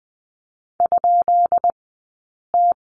Morse code, heard as a keyed tone, about 20 wpm, 700 Hz. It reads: ? T
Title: 190 Lesson 31d - Letter v - Two Characters 20wpm